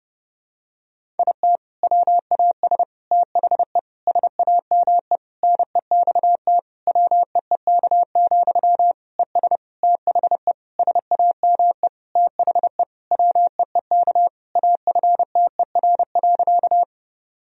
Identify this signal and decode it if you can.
Morse code, heard as a keyed tone, about 30 wpm, 700 Hz. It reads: IT WAS THE SAME NEXT WEEK, ES THE SAME THE WEEK AFTER.
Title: Sentences from Top 500 Words and Abbreviations 30wpm